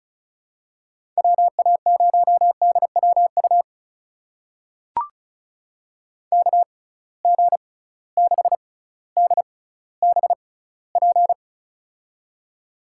Morse code, heard as a keyed tone, about 36 wpm, spaced as for 18 wpm, 700 Hz. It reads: WA0DWU KG6DBP
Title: USA Call Signs - 6 Characters - 6x Character Spacing - 35wpm